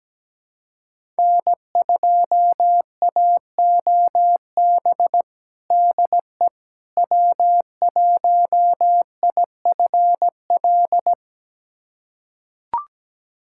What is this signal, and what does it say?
Morse code, heard as a keyed tone, about 17 wpm, 700 Hz. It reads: N2AOB DE W1IFL
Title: USA Callsign DE Callsign - 5 Characters - 17wpm